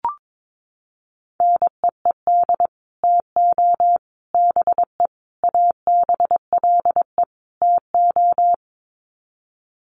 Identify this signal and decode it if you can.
Morse code, heard as a keyed tone, about 22 wpm, 700 Hz. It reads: NEED TO BE ABLE TO
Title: Sets of 5 Words Encore 22wpm